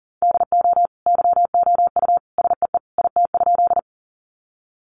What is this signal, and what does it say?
Morse code, heard as a keyed tone, about 40 wpm, 700 Hz. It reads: DO YOU SEE IT?
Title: Sentences from Top 100 Words 40wpm